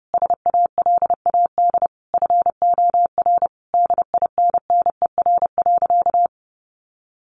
Morse code, heard as a keyed tone, about 30 wpm, 700 Hz. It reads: SALAD FOR DINNER.
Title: Easy Phrases 30wpm